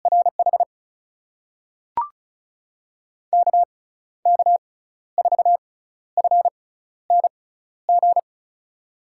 Morse code, heard as a keyed tone, about 36 wpm, spaced as for 14 wpm, 700 Hz. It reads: RH KK4FNG
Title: USA Call Signs - 6x Character Spacing - 35wpm